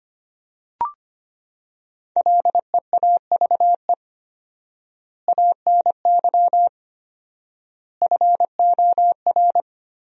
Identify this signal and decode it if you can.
Morse code, heard as a keyed tone, about 25 wpm, 700 Hz. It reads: LEAVE ANY FOR
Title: 4x Word-Spacing - Sentences from Top 400 Words - 25wpm